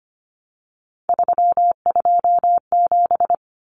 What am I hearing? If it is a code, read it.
Morse code, heard as a keyed tone, about 25 wpm, 700 Hz. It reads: 327
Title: Common Cut Numbers - 3-Digit 25wpm